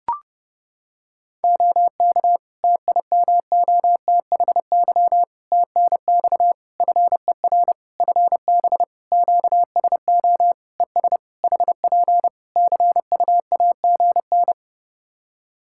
Morse code, heard as a keyed tone, about 30 wpm, 700 Hz. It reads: OK TIMOTHY TNX FER FB QSO ES HP CUAGN
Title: K1USN SST QSO Practice 30wpm